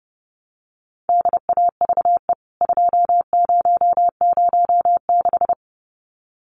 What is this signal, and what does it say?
Morse code, heard as a keyed tone, about 30 wpm, 700 Hz. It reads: DAVE 2006
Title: CWT Exchange 30wpm